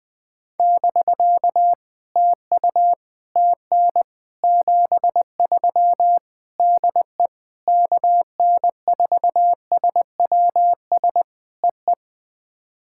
Morse code, heard as a keyed tone, about 21 wpm, 700 Hz. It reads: <BK> TU TN 73 DE KN4SWS EE
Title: POTA QSO Practice - 20wpm